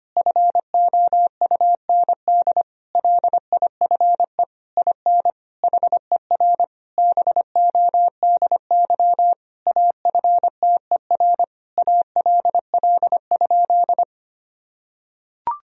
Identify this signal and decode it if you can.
Morse code, heard as a keyed tone, about 25 wpm, 700 Hz. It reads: FOUND LIFE IN HER BODY AFTER ALL?
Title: Sentences from Top 1000 Words - Review Part 2 - 25wpm